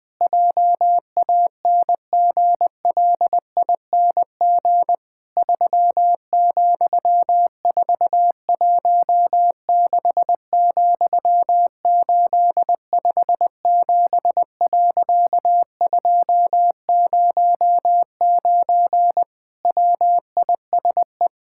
Morse code, heard as a keyed tone, about 20 wpm, 700 Hz. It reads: JANGLING 3,416,857.209 WISE
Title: Pangram Challenge v2 - 20wpm